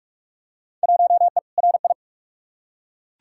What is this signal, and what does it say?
Morse code, heard as a keyed tone, about 46 wpm, 700 Hz. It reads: 1E RI
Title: Field Day Exchanges Encore - 45wpm